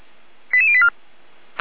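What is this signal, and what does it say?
Five-tone selective call (ZVEI), heard as quick runs of five tones, tones of about 70 ms. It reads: 88084
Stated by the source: ZVEI five-tone call received on SDR.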